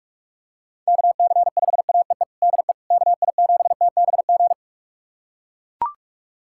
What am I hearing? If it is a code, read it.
Morse code, heard as a keyed tone, about 42 wpm, 700 Hz. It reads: KC5AEE DE KI7TBG
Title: USA Callsign DE Callsign - 6 Characters - 45wpm